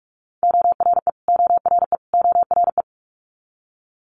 Morse code, heard as a keyed tone, about 46 wpm, 700 Hz. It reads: ORE ORE ORE
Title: Multi-letter Phonemes Repeated 3x 45wpm